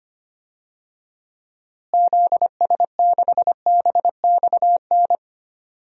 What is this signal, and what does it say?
Morse code, heard as a keyed tone, about 25 wpm, 700 Hz. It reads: ZS6BXN
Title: DXCC-100 Call Signs - 6 Characters - Repeated 3x - 25wpm